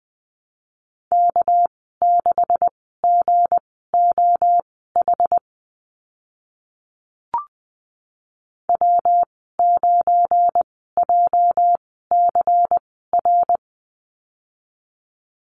Morse code, heard as a keyed tone, about 20 wpm, spaced as for 14 wpm, 700 Hz. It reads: K6GOH W9JCR
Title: 245 Lesson 39g - Letter z - Call Signs 20wpm